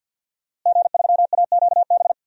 Morse code, heard as a keyed tone, about 52 wpm, 700 Hz. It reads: G3AQB
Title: DXCC-100 Call Signs - 5 Characters - Repeated 2x - 50wpm